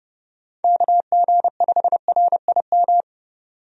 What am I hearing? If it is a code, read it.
Morse code, heard as a keyed tone, about 32 wpm, 700 Hz. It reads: KG5RIM
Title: USA Call Signs - 6 Characters - 30wpm